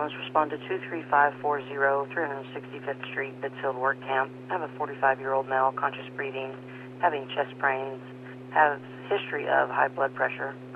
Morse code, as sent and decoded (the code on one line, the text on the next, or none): none